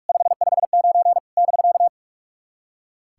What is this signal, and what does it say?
Morse code, heard as a keyed tone, about 44 wpm, 690 Hz. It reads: 559 <BK>
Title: SOTA QSO Practice 45wpm